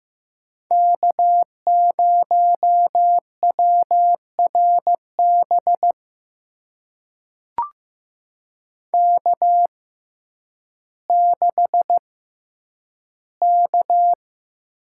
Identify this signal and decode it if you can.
Morse code, heard as a keyed tone, about 15 wpm, spaced as for 9 wpm, 700 Hz. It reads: K0WRB K6K
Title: USA Call Signs - 5 Characters - 6x Character Spacing - 15wpm